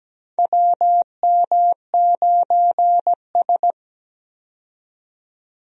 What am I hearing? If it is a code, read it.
Morse code, heard as a keyed tone, about 17 wpm, 700 Hz. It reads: WM9S
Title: USA Call Signs - 4 Characters - 17wpm